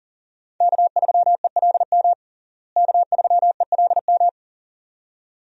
Morse code, heard as a keyed tone, about 40 wpm, 700 Hz. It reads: K3ELM K3ELM
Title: USA Call Signs - 5 Characters - Repeated 2x - 40wpm